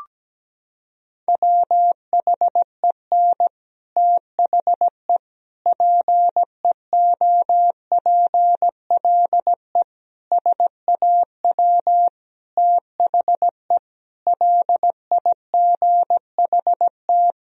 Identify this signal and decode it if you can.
Morse code, heard as a keyed tone, about 17 wpm, 700 Hz. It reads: WHEN THE PEOPLE SAW THE LIGHT
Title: Sentences from Top 1000 Words - Review Part 1 - 17wpm